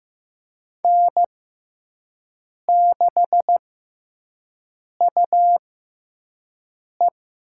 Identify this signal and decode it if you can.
Morse code, heard as a keyed tone, about 15 wpm, spaced as for 5 wpm, 700 Hz. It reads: N6UE
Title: USA Call Signs - 6x Character Spacing - 15wpm